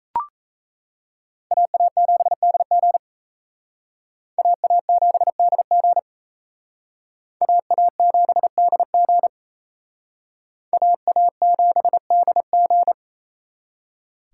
Morse code, heard as a keyed tone, about 34 wpm, 700 Hz. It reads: AA7DG AA7DG AA7DG AA7DG
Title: Speed Racing - USA 2x2 Call Signs 28wpm